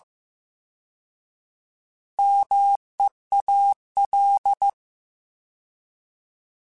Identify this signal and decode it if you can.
Morse code, heard as a keyed tone, about 15 wpm, 800 Hz. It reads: MEAL
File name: Alphabetical / Alphabetical mp3s / Ms @13